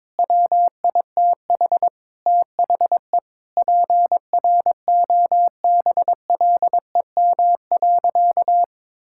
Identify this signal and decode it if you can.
Morse code, heard as a keyed tone, about 22 wpm, 700 Hz. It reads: WITH THE PROBLEM.
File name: practice / 22wpm / Sentences with 4-Word Phrases - 22wpm